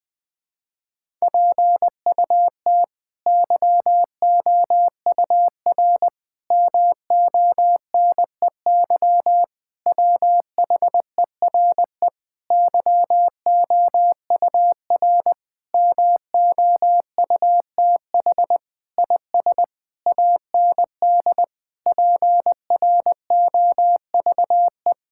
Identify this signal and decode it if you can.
Morse code, heard as a keyed tone, about 20 wpm, 700 Hz. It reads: PUT YOUR MONEY WHERE YOUR MOUTH IS AND PROVE
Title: Sentences with 8-Word Phrases - 20wpm